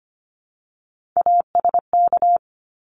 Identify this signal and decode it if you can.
Morse code, heard as a keyed tone, about 25 wpm, 700 Hz. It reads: ASK
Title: Top 300 Words Repeated 25wpm